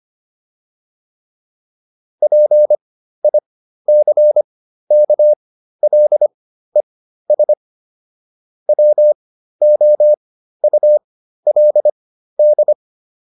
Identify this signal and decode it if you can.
Morse code, heard as a keyed tone, about 25 wpm, spaced as for 13 wpm, 600 Hz. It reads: PICKLES WOULD